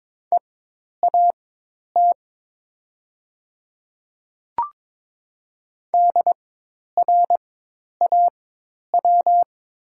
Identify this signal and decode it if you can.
Morse code, heard as a keyed tone, about 22 wpm, spaced as for 10 wpm, 700 Hz. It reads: EAT DRAW
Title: 4x Character Spacing - Top 300 Words - 22wpm